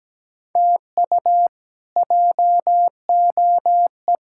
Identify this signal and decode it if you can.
Morse code, heard as a keyed tone, about 17 wpm, 700 Hz. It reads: TU JOE
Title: K1USN SST QSO Practice - 17wpm